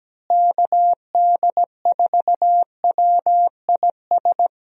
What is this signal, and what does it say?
Morse code, heard as a keyed tone, about 17 wpm, 700 Hz. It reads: KD4WIS
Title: USA Call Signs - 6 Characters - Repeated 2x - 17wpm